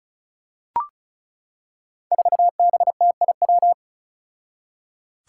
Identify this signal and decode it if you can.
Morse code, heard as a keyed tone, about 36 wpm, 700 Hz. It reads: 4BTIW
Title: Five Challenging Letter-Number Combinations 35wpm